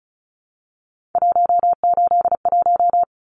Morse code, heard as a keyed tone, about 34 wpm, 700 Hz. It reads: 181
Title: Numbers - 3-Digit 35wpm